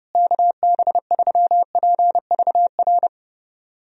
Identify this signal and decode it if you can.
Morse code, heard as a keyed tone, about 31 wpm, 700 Hz. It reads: KB3PVR
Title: USA Call Signs - 6 Characters - Repeated 2x - 30wpm